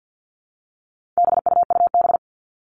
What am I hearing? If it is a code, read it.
Morse code, heard as a keyed tone, about 50 wpm, 700 Hz. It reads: BUUB
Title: All Cut Numbers - 4-Digit 50wpm